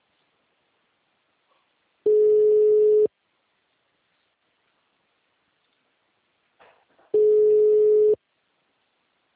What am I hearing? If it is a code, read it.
Ringback tone (Europe, 425 Hz).
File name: out- -2014- - - .668